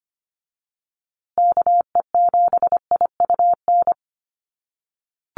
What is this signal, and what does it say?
Morse code, heard as a keyed tone, about 25 wpm, 700 Hz. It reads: KE7IUN
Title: USA Call Signs Repeated 3x 25wpm